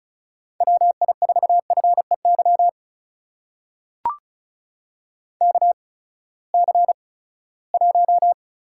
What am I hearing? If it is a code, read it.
Morse code, heard as a keyed tone, about 36 wpm, spaced as for 20 wpm, 700 Hz. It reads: WI4FEY KC1
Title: USA Call Signs - 6 Characters - 8x Character Spacing - 35wpm